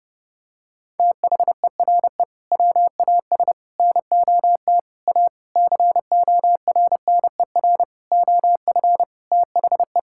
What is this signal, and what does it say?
Morse code, heard as a keyed tone, about 30 wpm, 700 Hz. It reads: THERE WAS NOT A CORNER OF THE
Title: Sentences from Top 1600 Words 30wpm